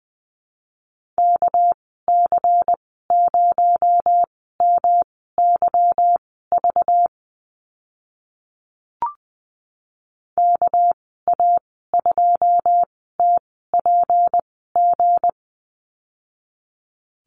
Morse code, heard as a keyed tone, about 20 wpm, spaced as for 14 wpm, 700 Hz. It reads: KC0MYV KA2TPG
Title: 238 Lesson 38g - Letter q - Call Signs 20wpm